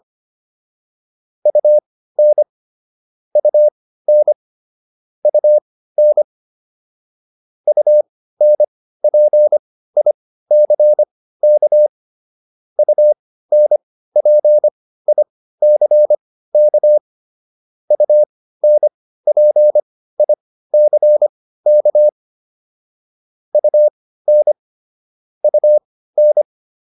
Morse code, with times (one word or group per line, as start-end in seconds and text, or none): 1.5-2.4 UN
3.4-4.3 UN
5.2-6.2 UN
7.7-11.9 UNPICK
12.8-17.0 UNPICK
17.9-22.1 UNPICK
23.5-24.5 UN
25.4-26.4 UN